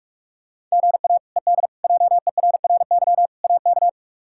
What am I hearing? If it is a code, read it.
Morse code, heard as a keyed tone, about 44 wpm, 700 Hz. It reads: GA ED JERRY AK
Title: K1USN SST QSO Practice - 45wpm